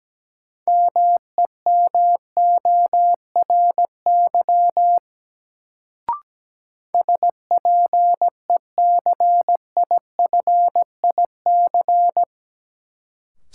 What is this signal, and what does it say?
Morse code, heard as a keyed tone, about 17 wpm, 700 Hz. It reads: MEMORY SPECIFIC
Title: Sentences from Top 1300 Words - New Word Drill Alt - 17wpm